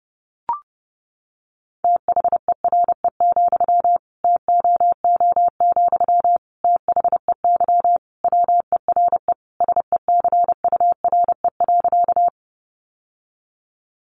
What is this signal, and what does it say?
Morse code, heard as a keyed tone, about 30 wpm, 700 Hz. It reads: THERE, TOO, THEY WERE SECURE.
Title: Sentences from Top 1200 Words 30wpm